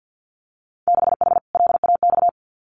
Morse code, heard as a keyed tone, about 50 wpm, 700 Hz. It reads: 6H LAX
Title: Winter Field Day Exchanges Encore 50wpm